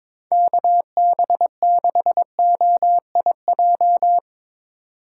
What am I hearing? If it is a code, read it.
Morse code, heard as a keyed tone, about 23 wpm, 700 Hz. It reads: KB6OIJ
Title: USA Call Signs - 6 Characters - 2x Character Spacing - 22wpm